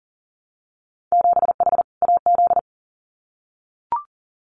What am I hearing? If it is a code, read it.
Morse code, heard as a keyed tone, about 40 wpm, 700 Hz. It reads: 7H AZ
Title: Winter Field Day Exchanges Encore 40wpm